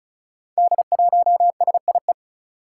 Morse code, heard as a keyed tone, about 36 wpm, 700 Hz. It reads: D1SIE
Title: Five Challenging Letter-Number Combinations 35wpm